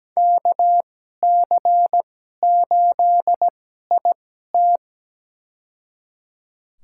Morse code, heard as a keyed tone, about 17 wpm, spaced as for 12 wpm, 700 Hz. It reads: KC8IT
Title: USA Call Signs - 5 Characters - 2x Character Spacing - 17wpm